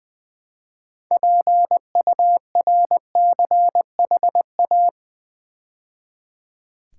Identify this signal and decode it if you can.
Morse code, heard as a keyed tone, about 20 wpm, 700 Hz. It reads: PURCHA
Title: Build A Word - 2 Letters At A Time - 8-Letter Words - 20wpm